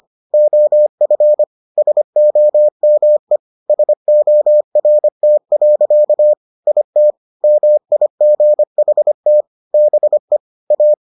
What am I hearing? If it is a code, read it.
Morse code, heard as a keyed tone, about 25 wpm, 600 Hz. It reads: OF SOME SORT. IT MIGHT BE A